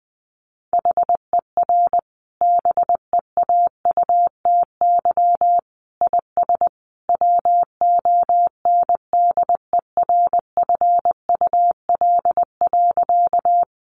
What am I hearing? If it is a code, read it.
Morse code, heard as a keyed tone, about 19 wpm, 700 Hz. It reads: HER BEAUTY IS WONDERFUL.
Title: Sentences from Top 1100 Words 20wpm